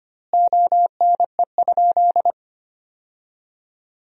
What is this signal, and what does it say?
Morse code, heard as a keyed tone, about 25 wpm, 700 Hz. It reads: ONE?
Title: 2x Word-Spacing - Sentences from Top 300 Words - 25wpm